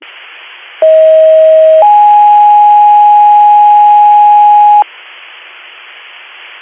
Two-tone page, heard as a one-second tone, then a three-second tone, about 640 then 830 Hz. It is heard from 0.8 to 4.8 s.